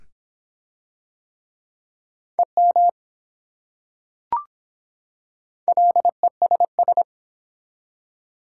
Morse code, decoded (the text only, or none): EM LESS